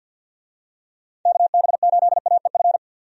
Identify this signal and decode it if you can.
Morse code, heard as a keyed tone, about 50 wpm, 700 Hz. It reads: KB8AEF
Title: USA Call Signs - 6 Characters - 50wpm